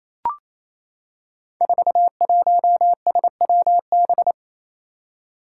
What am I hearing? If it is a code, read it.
Morse code, heard as a keyed tone, about 27 wpm, 700 Hz. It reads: 41SWB
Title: Five Challenging Letter-Number Combinations 28wpm